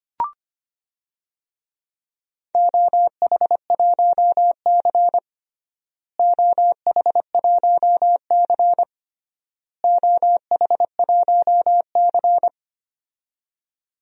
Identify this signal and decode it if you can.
Morse code, heard as a keyed tone, about 25 wpm, 700 Hz. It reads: OH1C OH1C OH1C
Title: DXCC-100 Call Signs - 4 Characters - Repeated 3x - 25wpm